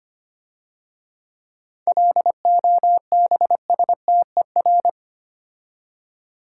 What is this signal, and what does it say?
Morse code, heard as a keyed tone, about 28 wpm, 700 Hz. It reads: LOBSTER
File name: Easy Words - Speed-Racing with Hints - 25wpm